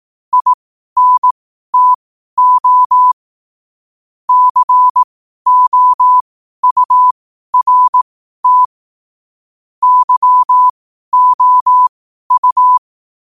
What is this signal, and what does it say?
Morse code, heard as a keyed tone, about 18 wpm, spaced as for 12 wpm, 1000 Hz. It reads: INTO COURT YOU
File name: qotdfiles / 12wpm